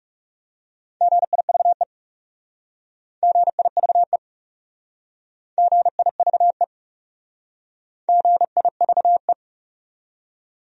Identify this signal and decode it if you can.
Morse code, heard as a keyed tone, about 40 wpm, 700 Hz. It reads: GIVE GIVE GIVE GIVE